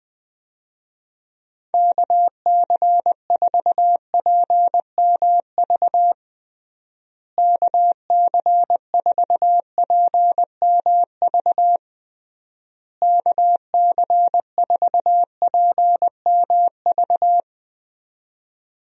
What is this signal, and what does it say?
Morse code, heard as a keyed tone, about 20 wpm, 700 Hz. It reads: KC4PMV KC4PMV KC4PMV
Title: USA Call Signs - 6 Characters - Repeated 3x - 20wpm